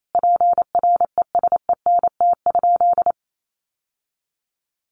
Morse code, heard as a keyed tone, about 28 wpm, 700 Hz. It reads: PRESENT?
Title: Sentences from Top 800 Words 28wpm